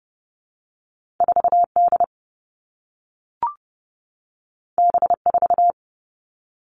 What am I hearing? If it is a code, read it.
Morse code, heard as a keyed tone, about 30 wpm, 700 Hz. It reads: 4 D B 4
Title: Two Challenging Letter-Number Combinations 30wpm